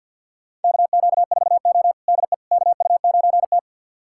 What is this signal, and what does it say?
Morse code, heard as a keyed tone, about 48 wpm, 700 Hz. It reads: KQ4O DE KU9T
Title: USA Callsign DE Callsign - 4 Characters - 50wpm